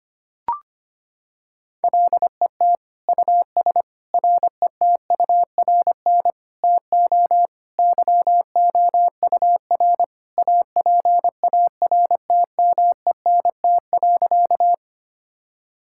Morse code, heard as a keyed tone, about 25 wpm, 700 Hz. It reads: LET US RETURN TO YOUR APARTMENT.
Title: Sentences from Top 1000 Words - Review Part 6 - 25wpm